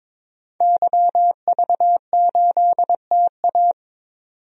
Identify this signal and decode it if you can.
Morse code, heard as a keyed tone, about 21 wpm, 700 Hz. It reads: YV8TA
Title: DXCC-100 Call Signs - 5 Characters - 22wpm